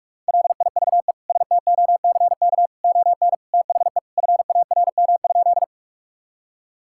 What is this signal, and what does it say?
Morse code, heard as a keyed tone, about 41 wpm, 700 Hz. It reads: LIVE STOCK ON THE FARM?
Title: Sentences from Top 1000 Words - Review Part 5 - 45wpm